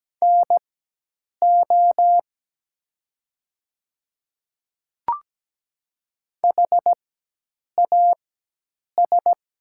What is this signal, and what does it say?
Morse code, heard as a keyed tone, about 17 wpm, spaced as for 8 wpm, 700 Hz. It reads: NO HAS
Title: 4x Character Spacing - Top 100 Words - 17wpm